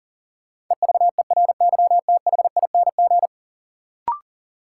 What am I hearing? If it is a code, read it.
Morse code, heard as a keyed tone, about 40 wpm, 700 Hz. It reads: EVERYTHING
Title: Sentences from Top 700 Words - New Word Drill - 40wpm